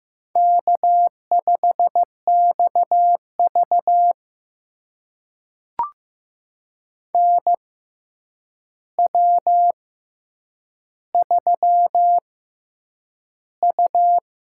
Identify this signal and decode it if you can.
Morse code, heard as a keyed tone, about 15 wpm, spaced as for 8 wpm, 700 Hz. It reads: K5XV NW3U
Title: USA Call Signs - 4 Characters - 6x Character Spacing - 15wpm